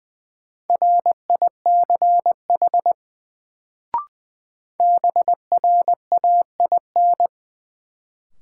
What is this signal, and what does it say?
Morse code, heard as a keyed tone, about 20 wpm, 700 Hz. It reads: RICH BRAIN
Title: Sentences from Top 1100 Words - New Word Drill Alt - 20wpm